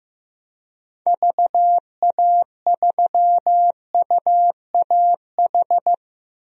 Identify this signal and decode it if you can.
Morse code, heard as a keyed tone, about 15 wpm, 700 Hz. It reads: VA3UAH
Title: DXCC-100 Call Signs - 6 Characters - Repeated 3x - 15wpm